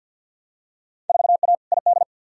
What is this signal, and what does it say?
Morse code, heard as a keyed tone, about 51 wpm, 700 Hz. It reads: 4A ID
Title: Field Day Exchanges - 50wpm